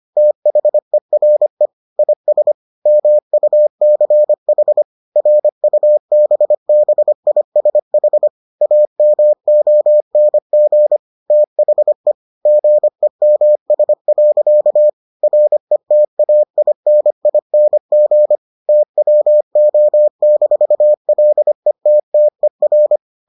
Morse code, heard as a keyed tone, about 25 wpm, 600 Hz. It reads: THERE IS MUCH RUBBISH AMONG THE GEMS. RETAINING TWO-LETTER